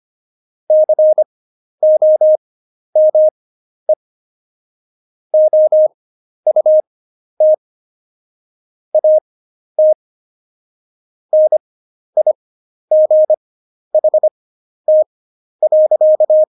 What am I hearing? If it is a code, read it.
Morse code, heard as a keyed tone, about 25 wpm, spaced as for 11 wpm, 620 Hz. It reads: COME OUT AT NIGHT.
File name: Jokes-25-char-10-eff